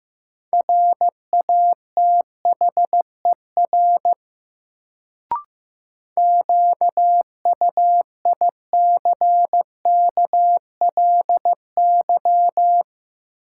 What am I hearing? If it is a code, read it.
Morse code, heard as a keyed tone, about 15 wpm, 700 Hz. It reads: RATHER QUICKLY
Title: Sentences from Top 700 Words - New Word Drill - 15wpm